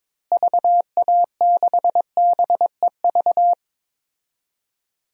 Morse code, heard as a keyed tone, about 22 wpm, 700 Hz. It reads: VA6BEV
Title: DXCC-100 Call Signs - 6 Characters - Repeated 3x - 22wpm